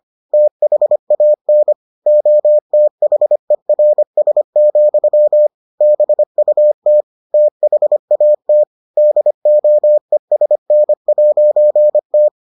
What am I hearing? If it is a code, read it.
Morse code, heard as a keyed tone, about 25 wpm, 600 Hz. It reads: THAN OTHERS, BUT THAT DOESN'T